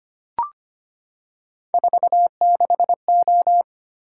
Morse code, heard as a keyed tone, about 25 wpm, 700 Hz. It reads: 46O
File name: Three Letter-Number Combinations - 25wpm